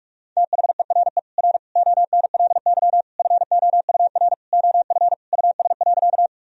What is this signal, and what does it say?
Morse code, heard as a keyed tone, about 46 wpm, 700 Hz. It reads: THERE R ONLY FOUR OF US.
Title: Sentences from Top 300 Words and Abbreviations 45wpm